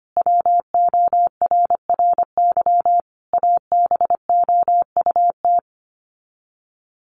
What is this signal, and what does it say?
Morse code, heard as a keyed tone, about 25 wpm, 700 Hz. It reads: WORRY ABOUT
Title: Sets of 5 Words 25wpm